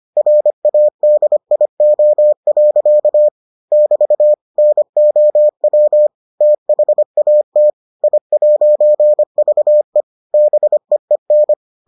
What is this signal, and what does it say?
Morse code, heard as a keyed tone, about 25 wpm, 600 Hz. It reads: RADIO. <BT> NOW THAT I'VE BEEN